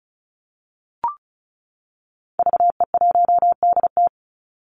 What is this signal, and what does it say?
Morse code, heard as a keyed tone, about 34 wpm, 700 Hz. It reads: VE1DT
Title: Call Signs - 35wpm